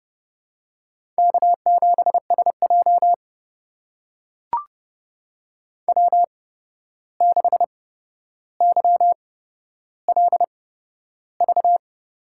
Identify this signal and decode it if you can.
Morse code, heard as a keyed tone, about 30 wpm, spaced as for 12 wpm, 700 Hz. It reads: K7SJ W6YLV
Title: USA Call Signs - 8x Character Spacing - 30wpm